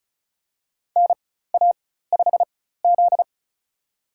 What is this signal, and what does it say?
Morse code, heard as a keyed tone, about 36 wpm, spaced as for 16 wpm, 700 Hz. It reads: NA5Z
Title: USA Call Signs - 4 Characters - 4x Character Spacing - 35wpm